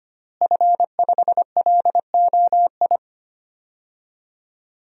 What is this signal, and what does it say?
Morse code, heard as a keyed tone, about 25 wpm, 700 Hz. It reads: F5LOI
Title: DXCC-100 Call Signs - 5 Characters - Repeated 3x - 25wpm